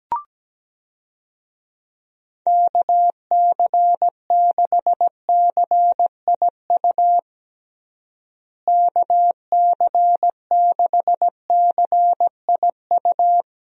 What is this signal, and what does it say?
Morse code, heard as a keyed tone, about 17 wpm, 700 Hz. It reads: KC6CIU KC6CIU
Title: USA Call Signs - 6 Characters - Repeated 3x - 17wpm